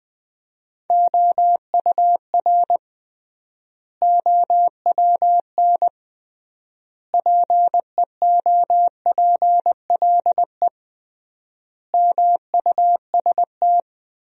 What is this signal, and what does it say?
Morse code, heard as a keyed tone, about 21 wpm, 700 Hz. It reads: OUR OWN PEOPLE MUST
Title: 3x Word-Spacing - Sentences from Top 300 Words - 20wpm